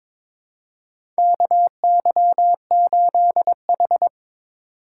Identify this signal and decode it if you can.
Morse code, heard as a keyed tone, about 22 wpm, 700 Hz. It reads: KY8H
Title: USA Call Signs - 4 Characters - 2x Character Spacing - 22wpm